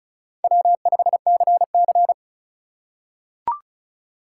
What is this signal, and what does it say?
Morse code, heard as a keyed tone, about 34 wpm, 700 Hz. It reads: W5CC
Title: USA Call Signs - 4 Characters - 6x Character Spacing - 35wpm